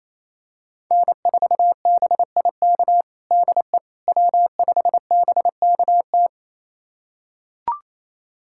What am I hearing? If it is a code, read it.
Morse code, heard as a keyed tone, about 28 wpm, 700 Hz. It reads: N4BIK DE W5BKT
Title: USA Callsign DE Callsign - 5 Characters - 28wpm